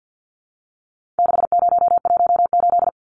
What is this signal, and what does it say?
Morse code, heard as a keyed tone, about 50 wpm, 700 Hz. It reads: 6018
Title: Numbers - 4-Digit 50wpm